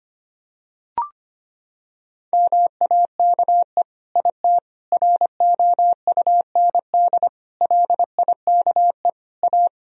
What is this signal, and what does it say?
Morse code, heard as a keyed tone, about 25 wpm, 700 Hz. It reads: MAKE IT ROUND LIKE A
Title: Sentences from Top 1000 Words - Review Part 3 - 25wpm